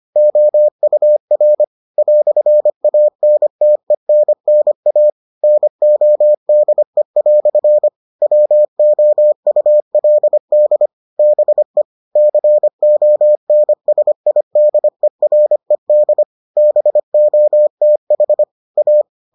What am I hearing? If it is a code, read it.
Morse code, heard as a keyed tone, about 25 wpm, 600 Hz. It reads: OUR "ANTENNA NODE" WOULD BE CONSIDERED BOTH A